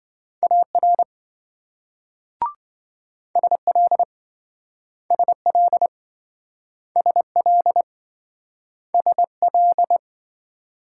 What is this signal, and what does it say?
Morse code, heard as a keyed tone, about 27 wpm, 700 Hz. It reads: AR SL SL SL SL